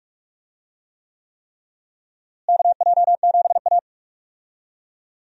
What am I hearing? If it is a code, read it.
Morse code, heard as a keyed tone, about 41 wpm, 700 Hz. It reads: KJ7A
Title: USA Call Signs - 4 Characters - Repeated 3x - 45wpm